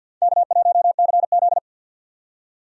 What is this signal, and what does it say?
Morse code, heard as a keyed tone, about 51 wpm, 700 Hz. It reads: K1CZ
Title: USA Call Signs - 4 Characters - 50wpm